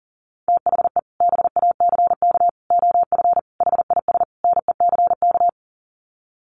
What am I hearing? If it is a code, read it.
Morse code, heard as a keyed tone, about 40 wpm, 700 Hz. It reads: THE BACK OF HIS NECK
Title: Sets of 5 Words Encore 40wpm